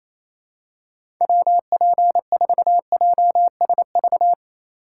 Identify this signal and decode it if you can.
Morse code, heard as a keyed tone, about 28 wpm, 700 Hz. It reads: WP4JSV
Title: USA Call Signs - 6 Characters - 3x Character Spacing - 28wpm